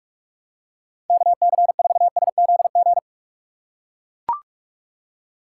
Morse code, heard as a keyed tone, about 44 wpm, 700 Hz. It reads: KC4SZG
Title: USA Call Signs - 6 Characters - 6x Character Spacing - 45wpm